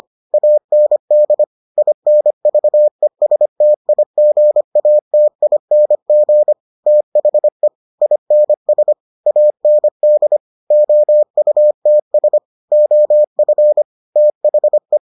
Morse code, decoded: AND INVESTIGATING THE INS AND OUTS OF THE

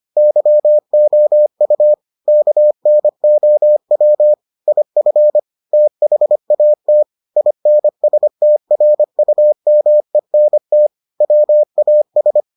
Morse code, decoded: YOU KNOW IF THAT INSTRUMENT WAS

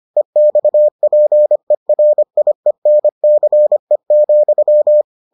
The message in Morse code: EXPERIENCE,

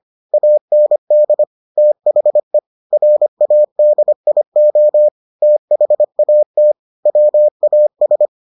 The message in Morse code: AND THE RADIO THAT WAS

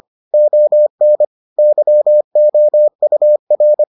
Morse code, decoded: ON YOUR

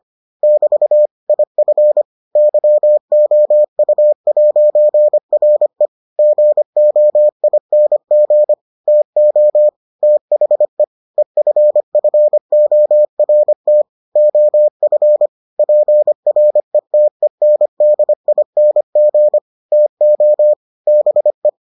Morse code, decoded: <BT> IF YOU'RE GOING TO THE EFFORT OF PRETENDING TO BE

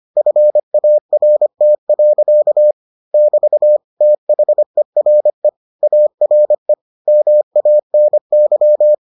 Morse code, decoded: FART. <BT> THERE ARE MANY

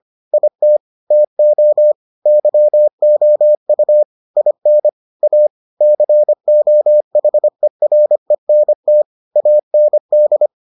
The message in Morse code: IT TO YOU IN A COHERENT AND

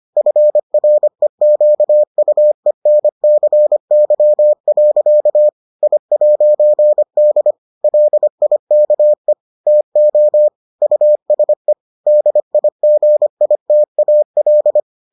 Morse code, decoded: FREQUENCY. I'D LIKE TO USE DIGITAL

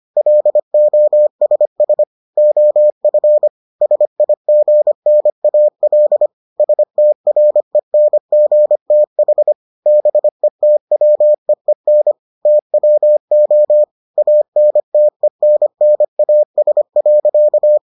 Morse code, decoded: LOSS OF SIGNAL STRENGTH BETWEEN TWO ANTENNAS.

